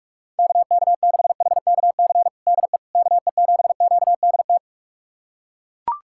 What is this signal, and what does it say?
Morse code, heard as a keyed tone, about 41 wpm, 700 Hz. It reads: KK6HKC DE KE7QDT
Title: USA Callsign DE Callsign - 6 Characters - 45wpm